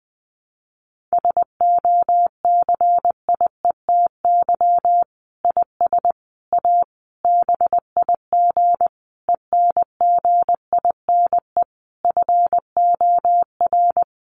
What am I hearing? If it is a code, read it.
Morse code, heard as a keyed tone, about 20 wpm, 700 Hz. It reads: SOCIETY IS A BIG ENGINE FOR
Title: Sentences from Top 1500 Words 20wpm